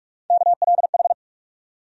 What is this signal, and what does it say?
Morse code, heard as a keyed tone, about 47 wpm, 700 Hz. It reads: KLH
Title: USA Call Sign Postfixes - 3 Letters 45wpm